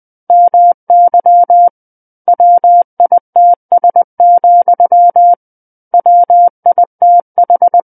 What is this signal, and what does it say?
Morse code, heard as a keyed tone, about 20 wpm, 700 Hz. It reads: MY WITS, WITH
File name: 20 WPM